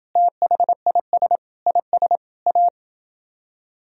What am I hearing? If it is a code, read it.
Morse code, heard as a keyed tone, about 30 wpm, 700 Hz. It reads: THIS IS A